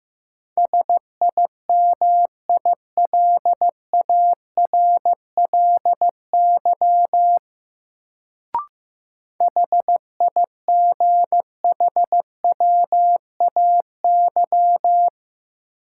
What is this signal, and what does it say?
Morse code, heard as a keyed tone, about 15 wpm, 700 Hz. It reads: SIMILARLY HIGHWAY
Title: Sentences from Top 1400 Words - New Word Drill - 15wpm